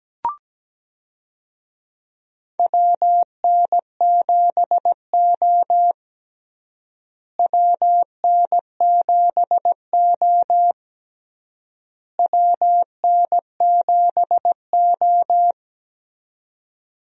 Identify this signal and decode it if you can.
Morse code, heard as a keyed tone, about 17 wpm, 700 Hz. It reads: WN7O WN7O WN7O
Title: USA Call Signs - 4 Characters - Repeated 3x - 17wpm